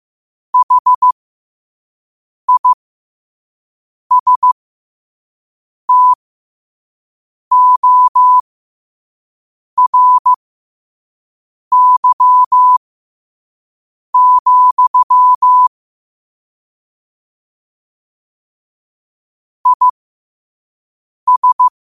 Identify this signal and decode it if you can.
Morse code, heard as a keyed tone, about 15 wpm, spaced as for 5 wpm, 1000 Hz. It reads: HISTORY, IS
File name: qotdfiles / 5wpm